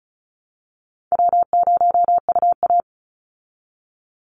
Morse code, heard as a keyed tone, about 34 wpm, 700 Hz. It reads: W0UA
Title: Easy Call Signs Repeated 3x - 35wpm